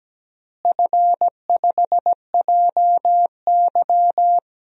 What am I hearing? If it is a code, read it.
Morse code, heard as a keyed tone, about 17 wpm, 700 Hz. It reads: F5JY
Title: DXCC-100 Call Signs - 4 Characters - Repeated 2x - 17wpm